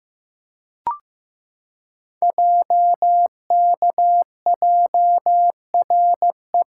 Morse code, heard as a keyed tone, about 15 wpm, 700 Hz. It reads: JKJRE
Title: Code Groups - 5 Letters - 15wpm